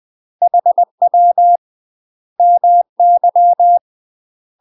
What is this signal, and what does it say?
Morse code, heard as a keyed tone, about 20 wpm, 700 Hz. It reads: HW MY
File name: UK MASTER QSOs TRAINING / 20WPM / UK MASTER QSOs 0028